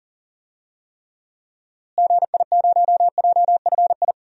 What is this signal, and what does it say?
Morse code, heard as a keyed tone, about 40 wpm, 700 Hz. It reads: GI0JFI
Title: DXCC-100 Call Signs - 6 Characters - Repeated 3x - 40wpm